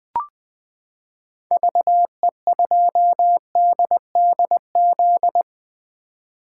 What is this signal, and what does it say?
Morse code, heard as a keyed tone, about 20 wpm, 700 Hz. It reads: VE2DDZ
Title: 2023 SOTA Chaser Call Signs - 20wpm